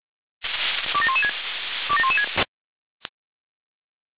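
Five-tone selective call (ZVEI), heard as quick runs of five tones, tones of about 70 ms. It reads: 27116 27116
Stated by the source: SDR-received ZVEI five-tone selective call.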